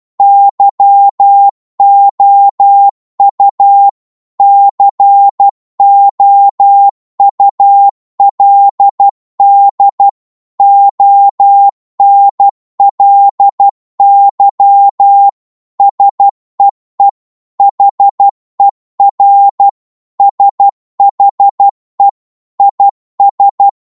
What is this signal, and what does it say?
Morse code, heard as a keyed tone, about 12 wpm, 800 Hz. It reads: YOU COULD ONLY SEE HER SHE IS
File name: Morse / psg-12wpm